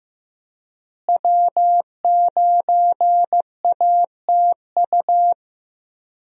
Morse code, heard as a keyed tone, about 15 wpm, 700 Hz. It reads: W9ATU
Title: USA Call Signs - 5 Characters - Repeated 3x - 15wpm